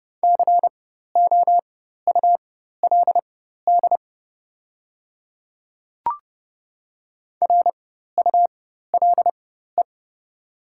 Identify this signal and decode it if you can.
Morse code, heard as a keyed tone, about 30 wpm, spaced as for 14 wpm, 700 Hz. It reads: COULD RULE
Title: 4x Character Spacing - Top 500 Words - 30wpm